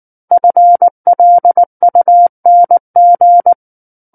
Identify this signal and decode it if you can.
Morse code, heard as a keyed tone, about 19 wpm, 700 Hz. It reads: FLUNG
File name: Book One / 10-Chapter 10 - 19 WPM